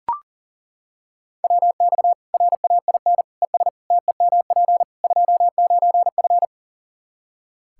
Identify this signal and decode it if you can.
Morse code, heard as a keyed tone, about 37 wpm, 700 Hz. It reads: WX RAIN ES TEMP 29F
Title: Practice QSO - Exchange 2 - Info - 40wpm